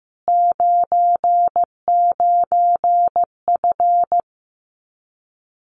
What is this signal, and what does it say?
Morse code, heard as a keyed tone, about 15 wpm, 700 Hz. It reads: 99F
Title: Common QSO Phrases Encore 15wpm